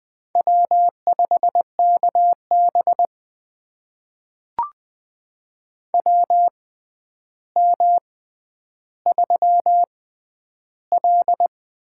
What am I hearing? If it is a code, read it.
Morse code, heard as a keyed tone, about 20 wpm, spaced as for 10 wpm, 700 Hz. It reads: W5KB WM3L
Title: USA Call Signs - 4 Characters - 6x Character Spacing - 20wpm